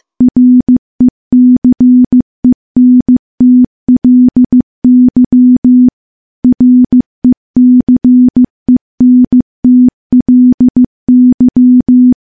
Morse code, heard as a keyed tone, about 16 wpm, 250 Hz. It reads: RECENTLY RECENTLY
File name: morse / morse-training-15WPM